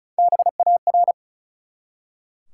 Morse code, heard as a keyed tone, about 36 wpm, 700 Hz. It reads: BAR
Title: Three Letter Words - 35wpm